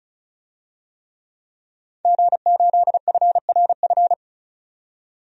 Morse code, heard as a keyed tone, about 35 wpm, 700 Hz. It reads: G8FRF
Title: DXCC-100 Call Signs - 5 Characters - 35wpm